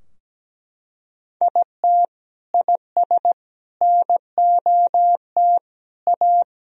Morse code, heard as a keyed tone, about 17 wpm, 700 Hz. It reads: IT IS NOT A
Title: Sentences from Top 1000 Words - Review Part 4 - 17wpm